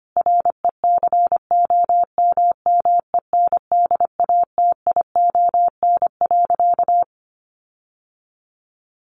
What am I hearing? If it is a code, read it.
Morse code, heard as a keyed tone, about 25 wpm, 700 Hz. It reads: RECOMMENDATION.
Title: Sentences from Top 1400 Words 25wpm